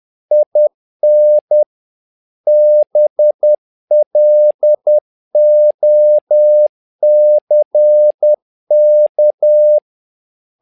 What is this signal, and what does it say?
Morse code, heard as a keyed tone, about 10 wpm, 600 Hz. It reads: IN BLOCK
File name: learning center / tower trouble / tower trouble 10wpm